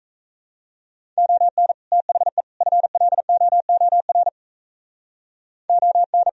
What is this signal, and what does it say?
Morse code, heard as a keyed tone, about 34 wpm, 700 Hz. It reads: ON THE FLOOR ON
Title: Speed-Racing - Sets of 3 Words 28wpm